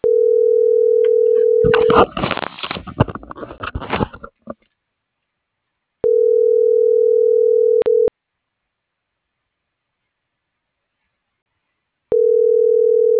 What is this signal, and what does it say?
Ringback tone (North America), 440 + 480 Hz.